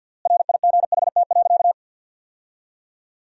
Morse code, heard as a keyed tone, about 51 wpm, 700 Hz. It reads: RIGHT.
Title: 3x Word-Spacing - Sentences from Top 500 Words - 50wpm